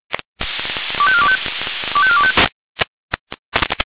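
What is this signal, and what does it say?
Five-tone selective call (ZVEI), heard as quick runs of five tones, tones of about 70 ms. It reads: 25425 25425
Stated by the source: ZVEI five-tone selective call received on an SDR.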